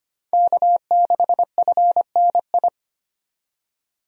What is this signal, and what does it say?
Morse code, heard as a keyed tone, about 25 wpm, 700 Hz. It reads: K6FNI
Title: USA Call Signs - 5 Characters - 25wpm